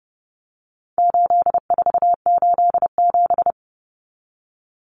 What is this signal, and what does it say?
Morse code, heard as a keyed tone, about 30 wpm, 700 Hz. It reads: 8487
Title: Common Cut Numbers - 4-Digit 30wpm